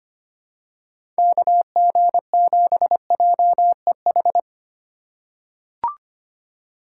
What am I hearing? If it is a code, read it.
Morse code, heard as a keyed tone, about 25 wpm, 700 Hz. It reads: KG7JEH
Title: USA Call Signs - 3x Character Spacing - 25wpm